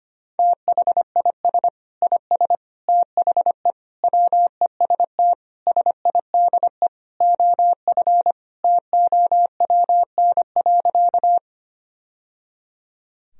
Morse code, heard as a keyed tone, about 25 wpm, 700 Hz. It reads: THIS IS THE WEST SIDE OF TOWN.
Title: Sentences from Top 1000 Words - Review Part 2 - 25wpm